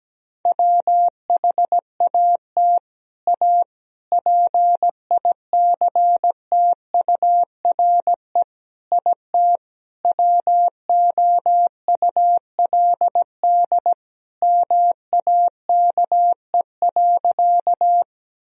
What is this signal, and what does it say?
Morse code, heard as a keyed tone, about 17 wpm, 700 Hz. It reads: WHAT A PICTURE IT WOULD MAKE.
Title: Sentences from Top 1000 Words - Review Part 1 - 17wpm